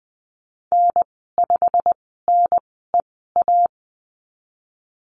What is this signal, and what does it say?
Morse code, heard as a keyed tone, about 20 wpm, spaced as for 15 wpm, 700 Hz. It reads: N5NEA
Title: 088 Lesson 15g - Number 5 - Call Signs 20wpm